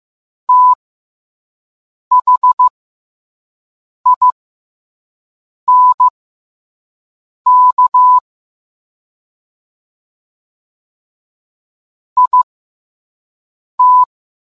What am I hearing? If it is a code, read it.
Morse code, heard as a keyed tone, about 15 wpm, spaced as for 5 wpm, 1000 Hz. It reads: THINK IT